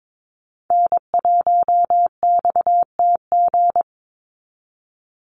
Morse code, heard as a keyed tone, about 22 wpm, 700 Hz. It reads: N1XTG
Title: USA Call Signs 22wpm